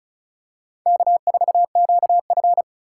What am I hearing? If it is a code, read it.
Morse code, heard as a keyed tone, about 34 wpm, 700 Hz. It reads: K4QF
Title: USA Call Signs - 3x Character Spacing - 35wpm